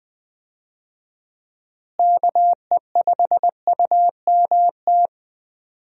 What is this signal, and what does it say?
Morse code, heard as a keyed tone, about 20 wpm, 700 Hz. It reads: KE5UMT
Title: USA Call Signs - 6 Characters - Repeated 2x - 20wpm